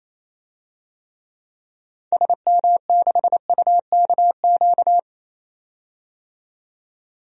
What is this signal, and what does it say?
Morse code, heard as a keyed tone, about 27 wpm, 700 Hz. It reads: SM6UKQ
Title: DXCC-100 Call Signs - 6 Characters - 28wpm